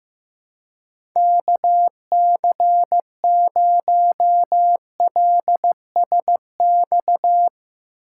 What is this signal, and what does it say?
Morse code, heard as a keyed tone, about 15 wpm, 700 Hz. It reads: KC0LSX
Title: USA Call Signs - 6 Characters - 6x Character Spacing - 15wpm